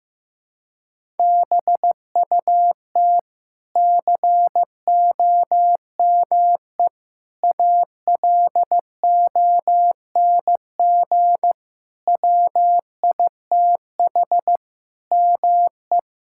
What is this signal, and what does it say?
Morse code, heard as a keyed tone, about 15 wpm, 700 Hz. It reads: BUT COME ALONG WITH ME
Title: Sentences from Top 1000 Words - Review Part 6 - 15wpm